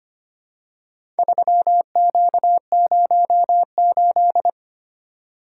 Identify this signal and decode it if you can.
Morse code, heard as a keyed tone, about 25 wpm, 700 Hz. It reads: 3Q08
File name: practice / 25wpm / Four Letter-Number Combinations - 25wpm